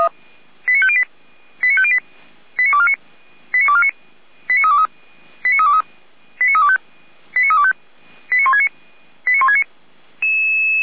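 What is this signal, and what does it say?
Five-tone selective call (ZVEI), heard as quick runs of five tones, tones of about 70 ms. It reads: 89508 89508 89249 89249 89323 89323 89325 89325 89169 89169
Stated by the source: ZVEI five-tone selective call 89171, received with an SDR.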